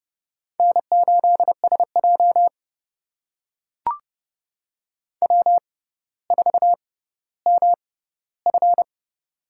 Morse code, heard as a keyed tone, about 31 wpm, spaced as for 16 wpm, 700 Hz. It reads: N8SJ W4MF
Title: USA Call Signs - 4 Characters - 6x Character Spacing - 30wpm